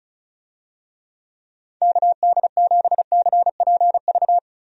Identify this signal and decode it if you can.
Morse code, heard as a keyed tone, about 36 wpm, 700 Hz. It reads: KD7CPV
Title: USA Call Signs - 6 Characters - Repeated 2x - 35wpm